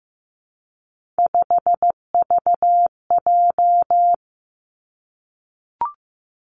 Morse code, heard as a keyed tone, about 15 wpm, 700 Hz. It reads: 5VJ
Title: Three Challenging Letter-Number Combinations 15wpm